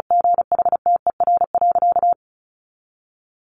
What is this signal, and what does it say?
Morse code, heard as a keyed tone, about 34 wpm, 700 Hz. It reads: GHTER.
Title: Sentences from Top 1500 Words 35wpm